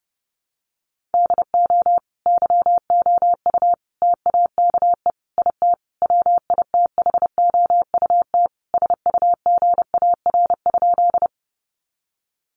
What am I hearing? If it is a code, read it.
Morse code, heard as a keyed tone, about 28 wpm, 700 Hz. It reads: DO YOU TAKE IT WITHOUT SUGAR?
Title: Sentences from Top 1200 Words 30wpm